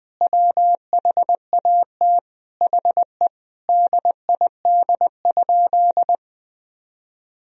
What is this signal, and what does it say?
Morse code, heard as a keyed tone, about 20 wpm, 700 Hz. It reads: WHAT HE DID?
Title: Sentences from Top 100 Words and Abbreviations 20wpm